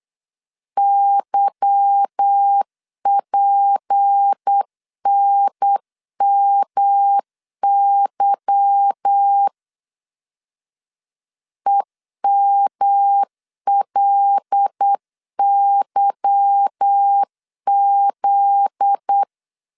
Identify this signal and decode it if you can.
Morse code, heard as a keyed tone, about 8 wpm, 800 Hz. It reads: YPNMY EMLYZ
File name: LEZIONI / 8 YJ45 / 8 Y 8wpm 3245